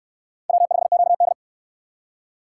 Morse code, heard as a keyed tone, about 69 wpm, 700 Hz. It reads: W5QD